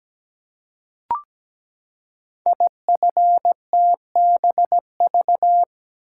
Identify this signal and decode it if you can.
Morse code, heard as a keyed tone, about 17 wpm, 700 Hz. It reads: IFTBV
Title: Code Groups - 5 Letters - 17wpm